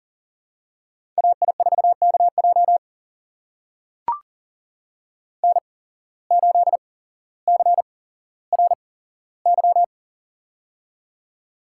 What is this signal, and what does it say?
Morse code, heard as a keyed tone, about 42 wpm, spaced as for 17 wpm, 700 Hz. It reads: AI4KJ N8CRY
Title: USA Call Signs - 5 Characters - 8x Character Spacing - 40wpm